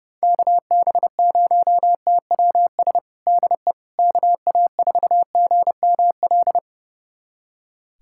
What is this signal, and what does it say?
Morse code, heard as a keyed tone, about 30 wpm, 700 Hz. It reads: KB0TWS DE KA4GML
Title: USA Callsign DE Callsign - 6 Characters - 30wpm